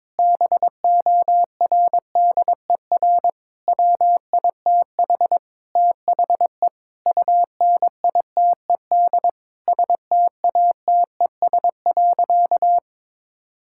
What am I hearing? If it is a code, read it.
Morse code, heard as a keyed tone, about 22 wpm, 700 Hz. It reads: BORDER WITH THE UNITED STATES.
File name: practice / 22wpm / Sentences with 4-Word Phrases - 22wpm